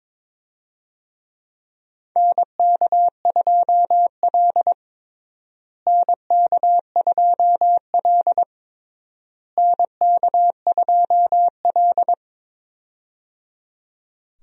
Morse code, heard as a keyed tone, about 22 wpm, 700 Hz. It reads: NK2L NK2L NK2L
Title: USA Call Signs - 4 Characters - Repeated 3x - 22wpm